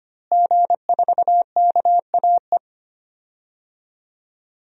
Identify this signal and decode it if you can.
Morse code, heard as a keyed tone, about 25 wpm, 700 Hz. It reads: G4KAE
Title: DXCC-100 Call Signs - 5 Characters - Repeated 2x - 25wpm